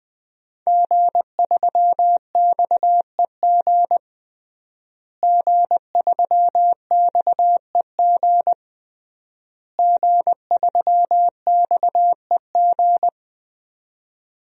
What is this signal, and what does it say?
Morse code, heard as a keyed tone, about 20 wpm, 700 Hz. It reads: G3XEG G3XEG G3XEG
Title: DXCC-100 Call Signs - 5 Characters - Repeated 3x - 20wpm